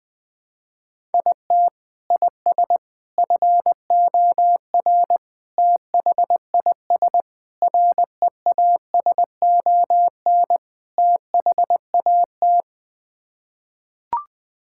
Morse code, heard as a keyed tone, about 20 wpm, 700 Hz. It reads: IT IS FOR THIS REASON THAT
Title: Sets of 6 Words - 20wpm